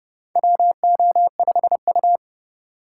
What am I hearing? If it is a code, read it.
Morse code, heard as a keyed tone, about 30 wpm, 700 Hz. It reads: WO5U
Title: USA Call Signs - 4 Characters - 2x Character Spacing - 30wpm